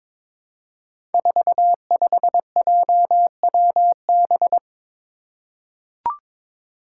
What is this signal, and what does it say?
Morse code, heard as a keyed tone, about 22 wpm, 700 Hz. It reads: 45JWB
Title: Five Challenging Letter-Number Combinations 22wpm